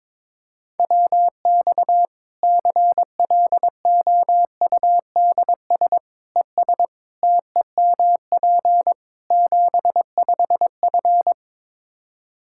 Morse code, heard as a keyed tone, about 21 wpm, 700 Hz. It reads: WX CLOUDS ES TEMP 75F
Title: Practice QSO - Exchange 2 - Info - 22wpm